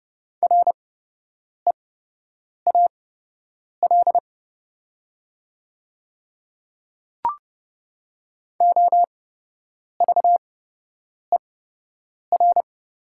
Morse code, heard as a keyed tone, about 30 wpm, spaced as for 8 wpm, 700 Hz. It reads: REAL OVER
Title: 8x Character Spacing - Top 300 Words - 30wpm